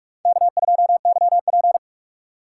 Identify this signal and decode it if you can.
Morse code, heard as a keyed tone, about 43 wpm, 700 Hz. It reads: K2YP
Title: USA Call Signs - 4 Characters - 6x Character Spacing - 45wpm